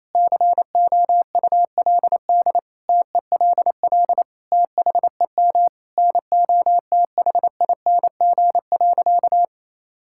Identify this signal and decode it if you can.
Morse code, heard as a keyed tone, about 28 wpm, 700 Hz. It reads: COULD TELL THEM NOTHING.
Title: Sentences from Top 1000 Words - Review Part 6 - 28wpm